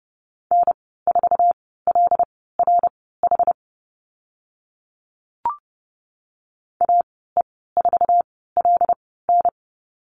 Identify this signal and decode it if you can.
Morse code, heard as a keyed tone, about 30 wpm, spaced as for 17 wpm, 700 Hz. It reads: N4LRH AE4LN
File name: Lesson 15g - Number 5 - Call Signs 30wpm